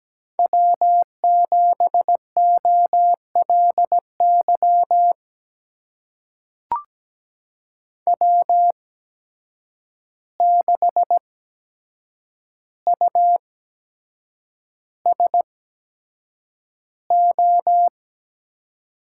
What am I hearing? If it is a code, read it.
Morse code, heard as a keyed tone, about 17 wpm, spaced as for 7 wpm, 700 Hz. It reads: W7OLY W6USO
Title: USA Call Signs - 5 Characters - 8x Character Spacing - 17wpm